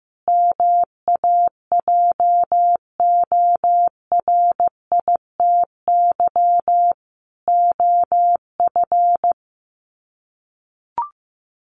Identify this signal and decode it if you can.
Morse code, heard as a keyed tone, about 15 wpm, 700 Hz. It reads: MAJORITY OF
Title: Sets of 3 Words 15wpm